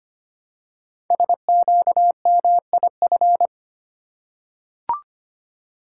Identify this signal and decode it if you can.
Morse code, heard as a keyed tone, about 25 wpm, 700 Hz. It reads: SQMIF
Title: Code Groups - 5 Letters - 25wpm